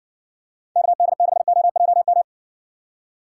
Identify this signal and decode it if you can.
Morse code, heard as a keyed tone, about 52 wpm, 700 Hz. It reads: KD6OJM